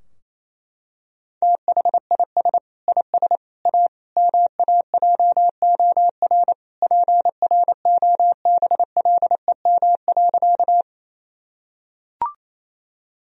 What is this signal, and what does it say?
Morse code, heard as a keyed tone, about 28 wpm, 700 Hz. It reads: THIS IS A MAJOR PROBLEM.
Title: Sentences from Top 1000 Words - Review Part 4 - 28wpm